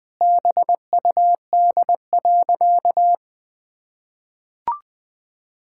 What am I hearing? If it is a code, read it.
Morse code, heard as a keyed tone, about 20 wpm, 700 Hz. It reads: BUD.
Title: Pangram Challenge v2 - 20wpm